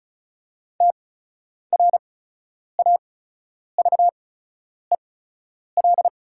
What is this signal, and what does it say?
Morse code, heard as a keyed tone, about 34 wpm, spaced as for 10 wpm, 690 Hz. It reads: TRAVEL
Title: 8x Character Spacing - Top 500 Words - 35wpm